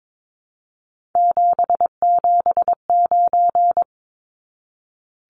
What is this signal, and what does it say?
Morse code, heard as a keyed tone, about 22 wpm, 700 Hz. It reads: 779
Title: Numbers - 3-Digit 22wpm